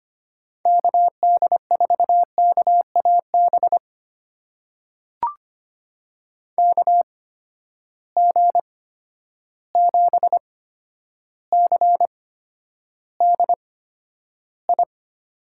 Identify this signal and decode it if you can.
Morse code, heard as a keyed tone, about 25 wpm, spaced as for 11 wpm, 700 Hz. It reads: KD4KAB KG7CDI
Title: USA Call Signs - 6 Characters - 8x Character Spacing - 25wpm